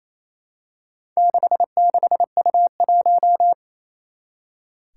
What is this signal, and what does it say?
Morse code, heard as a keyed tone, about 28 wpm, 700 Hz. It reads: 66U1
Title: Four Challenging Letter-Number Combinations 28wpm